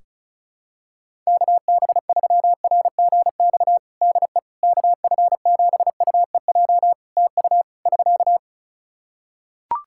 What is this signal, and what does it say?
Morse code, heard as a keyed tone, about 36 wpm, 700 Hz. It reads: KB3RGX DE KF7UEJ TU <SK>
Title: Practice QSO - Exchange 4 - Ending - 35wpm